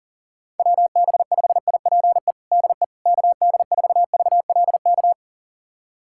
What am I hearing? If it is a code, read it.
Morse code, heard as a keyed tone, about 40 wpm, 700 Hz. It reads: WB5IPE DE KD4VLK
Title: USA Callsign DE Callsign 40wpm